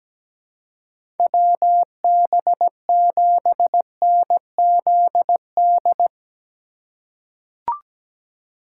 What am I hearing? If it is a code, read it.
Morse code, heard as a keyed tone, about 17 wpm, 700 Hz. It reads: WB7NZD
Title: USA Call Signs - 6 Characters - 2x Character Spacing - 17wpm